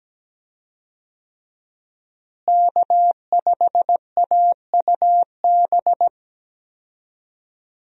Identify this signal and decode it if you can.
Morse code, heard as a keyed tone, about 17 wpm, 700 Hz. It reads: K5AUB
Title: USA Call Signs - 5 Characters - Repeated 2x - 17wpm